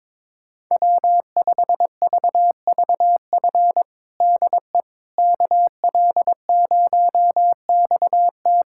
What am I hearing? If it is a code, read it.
Morse code, heard as a keyed tone, about 22 wpm, 700 Hz. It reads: W5VVF DE KL0XT
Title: USA Callsign DE Callsign - 5 Characters - 22wpm